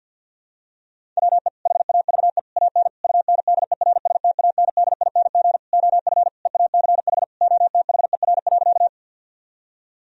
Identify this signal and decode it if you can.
Morse code, heard as a keyed tone, about 52 wpm, 700 Hz. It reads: WE HAVE AN UNDERSTANDING OF EACH OTHER.
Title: Sentences from Top 1000 Words - Review Part 3 - 50wpm